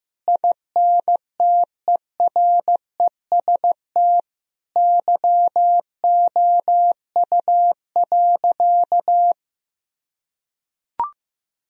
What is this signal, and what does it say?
Morse code, heard as a keyed tone, about 15 wpm, 700 Hz. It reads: INTEREST YOU.
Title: Sentences from Top 1000 Words - Review Part 4 - 15wpm